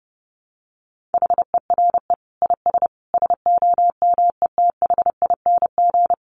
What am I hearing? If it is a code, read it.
Morse code, heard as a keyed tone, about 30 wpm, 700 Hz. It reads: HERE IS SOMETHING
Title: Sentences from Top 1000 Words 30wpm